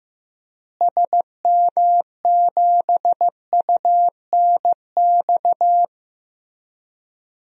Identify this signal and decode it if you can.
Morse code, heard as a keyed tone, about 15 wpm, 700 Hz. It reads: SM7UNX
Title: DXCC-100 Call Signs - 6 Characters - Repeated 3x - 15wpm